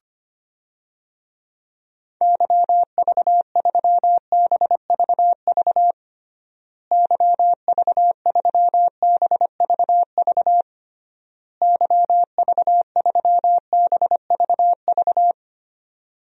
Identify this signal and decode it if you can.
Morse code, heard as a keyed tone, about 25 wpm, 700 Hz. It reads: YV3BVV YV3BVV YV3BVV
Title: DXCC-100 Call Signs - 6 Characters - Repeated 3x - 25wpm